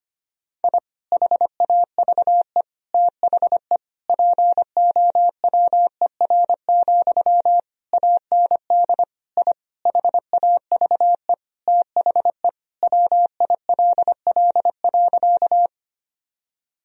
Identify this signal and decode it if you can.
Morse code, heard as a keyed tone, about 25 wpm, 700 Hz. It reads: I HAVE THE POWER, AND I HAVE THE WILL.
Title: Sentences from Top 1000 Words - Review Part 2 - 25wpm